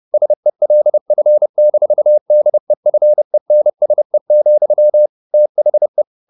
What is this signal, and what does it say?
Morse code, heard as a keyed tone, about 30 wpm, 600 Hz. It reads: SELF-DEFENSE, THE